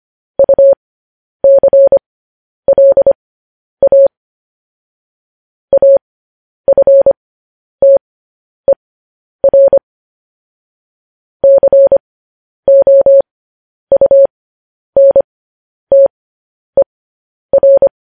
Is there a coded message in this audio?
Yes: Morse code from 0.4 to 17.9 s, heard as a keyed tone, about 25 wpm, spaced as for 10 wpm, 550 Hz. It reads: UCLA AFTER COUNTER